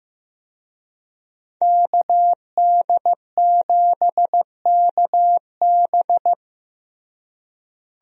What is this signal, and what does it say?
Morse code, heard as a keyed tone, about 16 wpm, 700 Hz. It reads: KD7KB
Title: USA Call Signs - 5 Characters - Repeated 2x - 15wpm